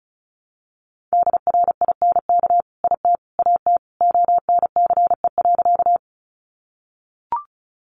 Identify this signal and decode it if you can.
Morse code, heard as a keyed tone, about 34 wpm, 700 Hz. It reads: DRINK IT AT ONCE.
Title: Sentences from Top 1500 Words 35wpm